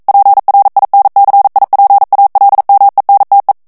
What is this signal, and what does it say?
Morse code, heard as a keyed tone, about 42 wpm, 800 Hz. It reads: PRINCIPALMENTE
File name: morse mp3 / ITALIEN texte clair 1 vitesse 40 mots minute